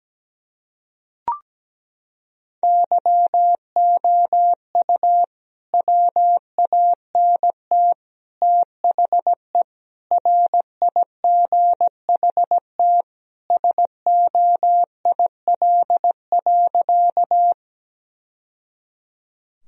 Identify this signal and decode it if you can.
Morse code, heard as a keyed tone, about 17 wpm, 700 Hz. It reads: YOU WANT THE RIGHT SOIL.
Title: Sentences from Top 1000 Words - Review Part 4 - 17wpm